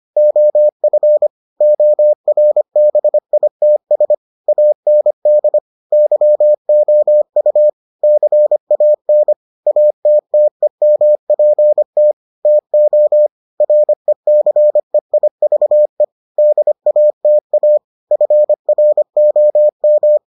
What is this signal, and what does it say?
Morse code, heard as a keyed tone, about 25 wpm, 600 Hz. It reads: OF ORBITS AND YOU CAN ATTEMPT TO RECEIVE DATA FROM